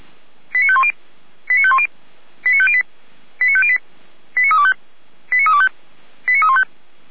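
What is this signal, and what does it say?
Five-tone selective call (ZVEI), heard as quick runs of five tones, tones of about 70 ms. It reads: 89510 89510 89508 89508 89325 89325 89315
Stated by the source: ZVEI five-tone selective call received by SDR, from feuerwehr-orlamuende.org (fire brigade Orlamünde) audio server.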